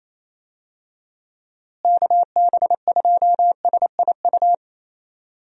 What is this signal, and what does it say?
Morse code, heard as a keyed tone, about 28 wpm, 700 Hz. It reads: KB2SIU
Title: USA Call Signs - 6 Characters - Repeated 2x - 28wpm